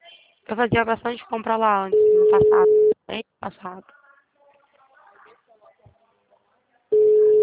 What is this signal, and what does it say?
Ringback tone (Europe, 425 Hz), with speech.
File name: out- -2026- - - .234